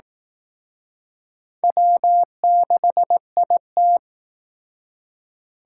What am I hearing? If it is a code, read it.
Morse code, heard as a keyed tone, about 18 wpm, 700 Hz. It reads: W6IT
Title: Speed Racing - USA 1x2 Call Signs 15wpm